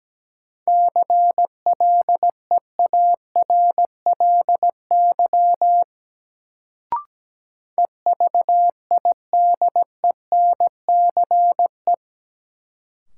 Morse code, heard as a keyed tone, about 17 wpm, 700 Hz. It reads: CLEARLY EVIDENCE
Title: Sentences from Top 1300 Words - New Word Drill Alt - 17wpm